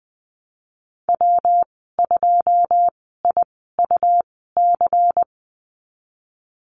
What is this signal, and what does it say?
Morse code, heard as a keyed tone, about 19 wpm, spaced as for 14 wpm, 700 Hz. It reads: W2IUC
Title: 154 Lesson 25g - Letter p - Call Signs 20wpm